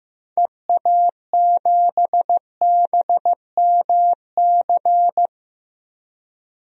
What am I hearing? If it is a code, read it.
Morse code, heard as a keyed tone, about 14 wpm, 700 Hz. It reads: EA7BMC
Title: DXCC-100 Call Signs - 6 Characters - Repeated 2x - 15wpm